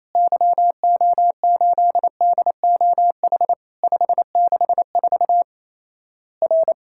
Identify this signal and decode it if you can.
Morse code, heard as a keyed tone, about 28 wpm, 700 Hz. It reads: YO8DOH 564 R
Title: Worked All Europe DX Contest - QTC Exchange - 28wpm